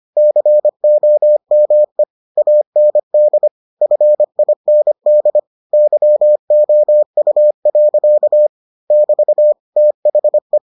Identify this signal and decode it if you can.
Morse code, heard as a keyed tone, about 25 wpm, 600 Hz. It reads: COME AND FIND YOU. <BT> THE